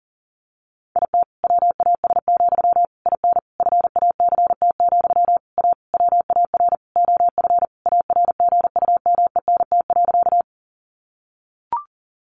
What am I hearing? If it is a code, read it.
Morse code, heard as a keyed tone, about 40 wpm, 700 Hz. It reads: IT WAS, IN FACT, A WAR OF ARGUMENT.
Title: Sentences from Top 1400 Words 40wpm